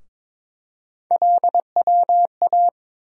Morse code, heard as a keyed tone, about 22 wpm, 700 Hz. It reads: LWA
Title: USA Call Sign Postfixes - 3 Letters 22wpm